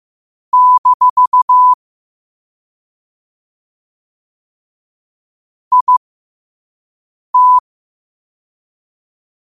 Morse code, heard as a keyed tone, about 15 wpm, spaced as for 5 wpm, 1000 Hz. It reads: - IT
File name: qotdfiles / 5wpm